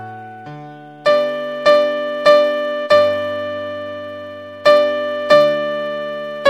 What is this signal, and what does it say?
There is no code.